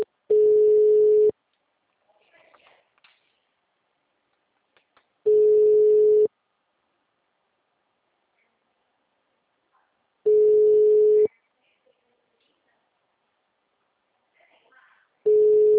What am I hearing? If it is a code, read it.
Ringback tone (Europe, 425 Hz).